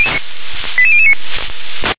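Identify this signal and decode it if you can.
Five-tone selective call (ZVEI), heard as quick runs of five tones, tones of about 70 ms. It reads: 88009
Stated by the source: ZVEI five-tone selective call received on an SDR.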